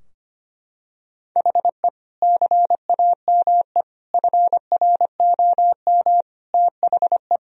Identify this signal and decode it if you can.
Morse code, heard as a keyed tone, about 25 wpm, 700 Hz. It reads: HE CAME FROM THE
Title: Sentences from Top 1000 Words - Review Part 2 - 25wpm